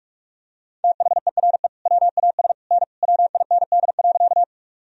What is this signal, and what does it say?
Morse code, heard as a keyed tone, about 46 wpm, 700 Hz. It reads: THERE WAS N WIND.
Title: Sentences from Top 400 Words and Abbreviations 45wpm